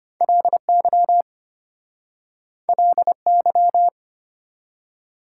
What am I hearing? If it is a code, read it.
Morse code, heard as a keyed tone, about 33 wpm, 690 Hz. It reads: LY LY